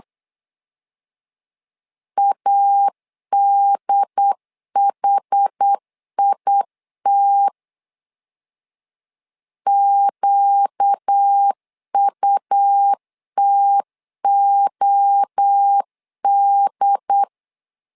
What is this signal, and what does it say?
Morse code, heard as a keyed tone, about 8 wpm, 800 Hz. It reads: ADHIT QUTOD
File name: LEZIONI / 4 Q 8wpm 2732